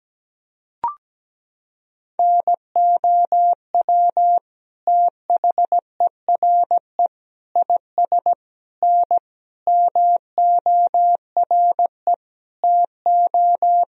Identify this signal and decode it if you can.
Morse code, heard as a keyed tone, about 17 wpm, 700 Hz. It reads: NOW THERE IS N MORE TO
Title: Sentences from Top 100 Words and Abbreviations 17wpm